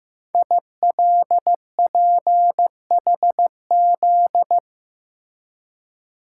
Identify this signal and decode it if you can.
Morse code, heard as a keyed tone, about 15 wpm, 700 Hz. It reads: ILPHZ
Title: Code Groups - 5 Letter:Numbers - 15wpm